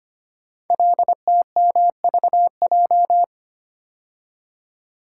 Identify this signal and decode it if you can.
Morse code, heard as a keyed tone, about 25 wpm, 700 Hz. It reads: LTMVJ
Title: Code Groups - 5 Letters - 25wpm